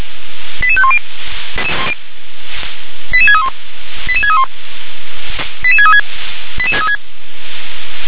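Five-tone selective call (ZVEI), heard as quick runs of five tones, tones of about 70 ms. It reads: 88510 88631 88631 88636
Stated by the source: ZVEI five-tone call received on SDR.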